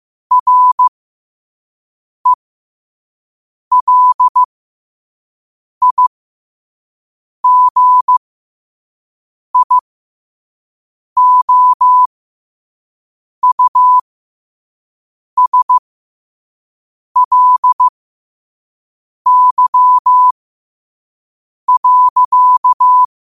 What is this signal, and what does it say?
Morse code, heard as a keyed tone, about 15 wpm, spaced as for 5 wpm, 1000 Hz. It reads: RELIGIOUSLY.
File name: qotdfiles / 5wpm